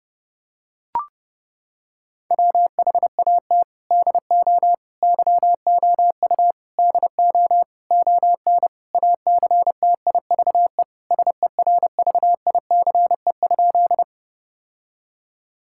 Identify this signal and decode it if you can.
Morse code, heard as a keyed tone, about 30 wpm, 700 Hz. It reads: WHAT DO YOU DO ON ACTIVE SERVICE?
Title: Sentences from Top 1000 Words - Review Part 6 - 30wpm